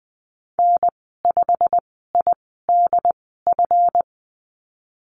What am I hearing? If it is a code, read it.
Morse code, heard as a keyed tone, about 19 wpm, spaced as for 14 wpm, 700 Hz. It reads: N5IDF
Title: 154 Lesson 25g - Letter p - Call Signs 20wpm